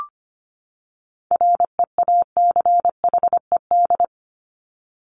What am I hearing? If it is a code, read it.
Morse code, heard as a keyed tone, about 25 wpm, 700 Hz. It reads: REACHED
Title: ED Suffix - Top 100 Words with Base Words 25wpm